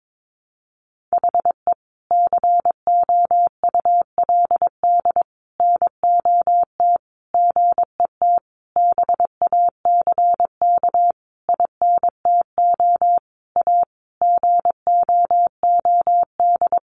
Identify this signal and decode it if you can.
Morse code, heard as a keyed tone, about 22 wpm, 700 Hz. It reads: HE COULD NOT GET BACK INTO A GOOD
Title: Sentences from Top 1100 Words 22wpm